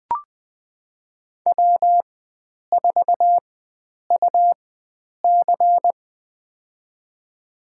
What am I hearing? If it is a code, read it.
Morse code, heard as a keyed tone, about 20 wpm, spaced as for 9 wpm, 700 Hz. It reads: W4UC
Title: USA Call Signs - 4x Character Spacing - 20wpm